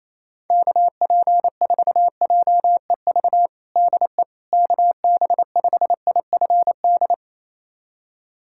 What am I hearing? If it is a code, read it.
Morse code, heard as a keyed tone, about 28 wpm, 700 Hz. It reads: KP4JEV DE KB5IFD
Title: USA Callsign DE Callsign 28wpm